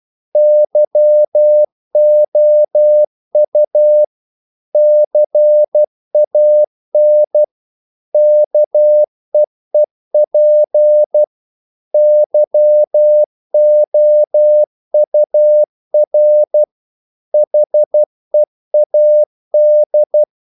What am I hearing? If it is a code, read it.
Morse code, heard as a keyed tone, about 12 wpm, 600 Hz. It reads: YOU CAN KEEP YOUR HEAD
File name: CW / Practice / If12